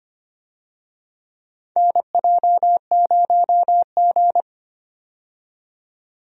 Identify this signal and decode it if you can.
Morse code, heard as a keyed tone, about 25 wpm, 700 Hz. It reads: NJ0G
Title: USA Call Signs - 4 Characters - 25wpm